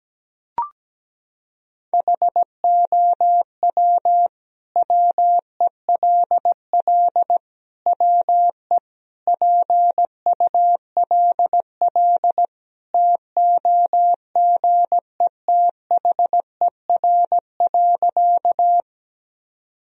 Morse code, heard as a keyed tone, about 17 wpm, 700 Hz. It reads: HOW WELL WE PULL TOGETHER.
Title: Sentences from Top 1000 Words - Review Part 2 - 17wpm